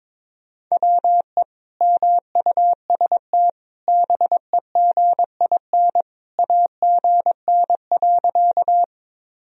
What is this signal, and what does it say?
Morse code, heard as a keyed tone, about 22 wpm, 700 Hz. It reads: WE MUST BEGIN AGN.
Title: Sentences from Top 300 Words and Abbreviations 22wpm